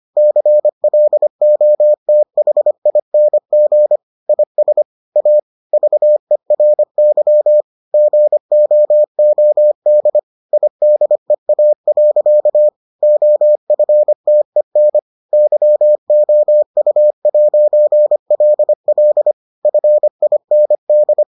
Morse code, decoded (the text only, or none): CLOTHING IS A VERY GOOD IDEA. OFTEN YOU'LL FIND